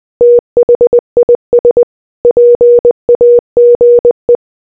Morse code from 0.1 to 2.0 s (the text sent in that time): THIS